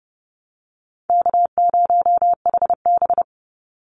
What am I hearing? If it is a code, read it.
Morse code, heard as a keyed tone, about 30 wpm, 700 Hz. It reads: K0HB
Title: Call Signs Repeated 2x - 30wpm